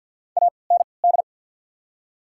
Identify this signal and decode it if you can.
Morse code, heard as a keyed tone, about 48 wpm, spaced as for 28 wpm, 700 Hz. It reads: AND
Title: 3x Character Spacing - Top 100 Words - 50wpm